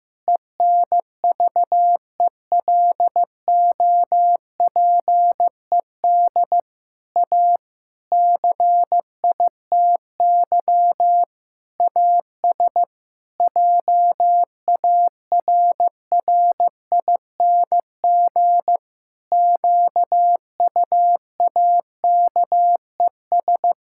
Morse code, heard as a keyed tone, about 15 wpm, 700 Hz. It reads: ENVELOPED A CITY AS JARRING QUAKES
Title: Pangram Challenge v2 - 15wpm